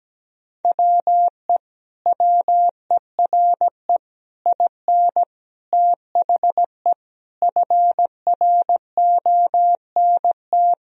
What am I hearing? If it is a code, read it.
Morse code, heard as a keyed tone, about 17 wpm, 700 Hz. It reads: WE WERE IN THE FRONT
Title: Sentences from Top 1000 Words - Review Part 2 - 17wpm